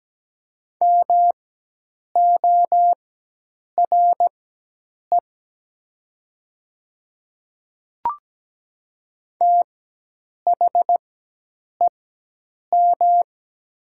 Morse code, heard as a keyed tone, about 17 wpm, spaced as for 8 wpm, 700 Hz. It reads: MORE THEM
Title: 4x Character Spacing - Top 500 Words - 17wpm